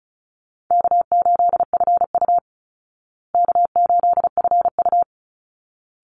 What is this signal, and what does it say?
Morse code, heard as a keyed tone, about 34 wpm, 700 Hz. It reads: K8FU K8FU
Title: Easy Call Signs Repeated 2x - 35wpm